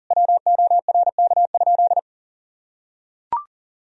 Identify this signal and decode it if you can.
Morse code, heard as a keyed tone, about 37 wpm, 700 Hz. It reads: WORK?
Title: Sentences from Top 1000 Words - Review Part 3 - 40wpm